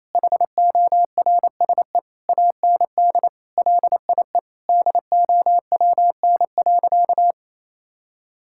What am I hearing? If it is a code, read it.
Morse code, heard as a keyed tone, about 28 wpm, 700 Hz. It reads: HORSE AND LIE DOWN.
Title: Sentences from Top 1600 Words 28wpm